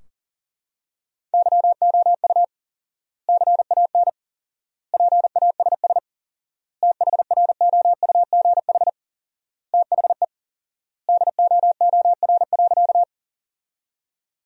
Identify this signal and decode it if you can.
Morse code, heard as a keyed tone, about 40 wpm, 700 Hz. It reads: YOU CAN PASS THROUGH THE DOOR.
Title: 4x Word-Spacing - Sentences from Top 400 Words - 40wpm